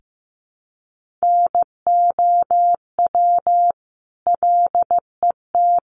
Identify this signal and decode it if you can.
Morse code, heard as a keyed tone, about 15 wpm, 700 Hz. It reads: NOW LET
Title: Sentences from Top 1500 Words 15wpm